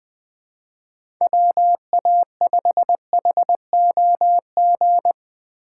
Morse code, heard as a keyed tone, about 20 wpm, 700 Hz. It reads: WA5HOG